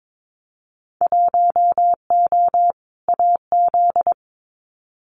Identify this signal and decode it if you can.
Morse code, heard as a keyed tone, about 22 wpm, 700 Hz. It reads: 1O AZ
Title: Winter Field Day Exchanges 22wpm